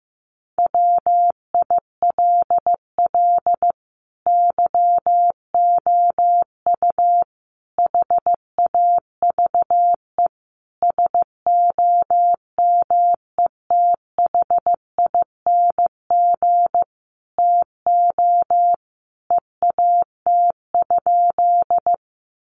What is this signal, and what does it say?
Morse code, heard as a keyed tone, about 15 wpm, 700 Hz. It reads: WILL YOU HAVE SOMETHING TO EAT?
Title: Sentences from Top 600 Words 15wpm